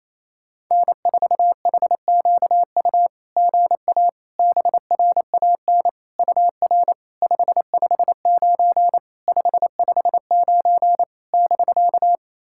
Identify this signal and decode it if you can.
Morse code, heard as a keyed tone, about 28 wpm, 660 Hz. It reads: N4HQU GA BRAN UR 559 559 <BK>
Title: SOTA QSO Practice 28wpm